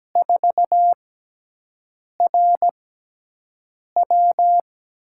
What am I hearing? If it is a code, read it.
Morse code, heard as a keyed tone, about 17 wpm, spaced as for 6 wpm, 700 Hz. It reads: 4RW
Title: USA Call Signs - 4 Characters - 6x Character Spacing - 17wpm